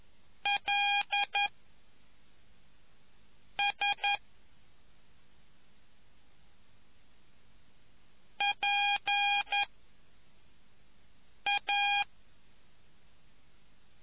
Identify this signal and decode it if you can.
Morse code, heard as a keyed tone, about 11 wpm, 2450 Hz. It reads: L S P A